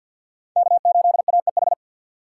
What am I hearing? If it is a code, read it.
Morse code, heard as a keyed tone, about 48 wpm, 700 Hz. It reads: K8AEH
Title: USA Call Signs - 5 Characters - 4x Character Spacing - 50wpm